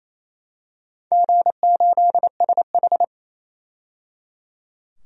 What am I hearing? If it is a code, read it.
Morse code, heard as a keyed tone, about 28 wpm, 700 Hz. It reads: G8SH
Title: DXCC-100 Call Signs - 4 Characters - 28wpm